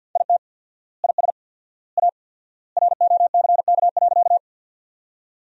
Morse code, heard as a keyed tone, about 50 wpm, 700 Hz. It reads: IT IS A ROCK.
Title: 4x Word-Spacing - Sentences from Top 400 Words - 50wpm